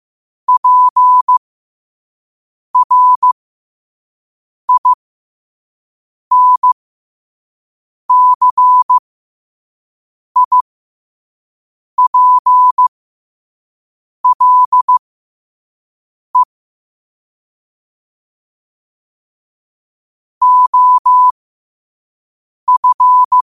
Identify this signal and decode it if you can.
Morse code, heard as a keyed tone, about 15 wpm, spaced as for 5 wpm, 1000 Hz. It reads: PRINCIPLE OF
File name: qotdfiles / 5wpm